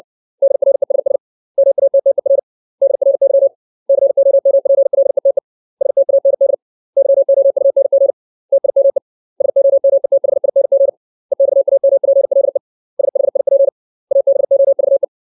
Morse code, heard as a keyed tone, about 60 wpm, 550 Hz. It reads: DRESS MATTER DRY COMPLETE STAND YOUNG TIME SOMETHING EXAMPLE SHIP ABOVE
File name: 632 words / 632 Words 60WPM